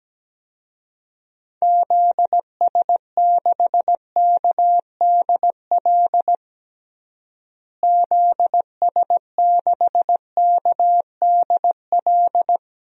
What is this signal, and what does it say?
Morse code, heard as a keyed tone, about 17 wpm, 700 Hz. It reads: ZS6KDL ZS6KDL
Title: DXCC-100 Call Signs - 6 Characters - Repeated 2x - 17wpm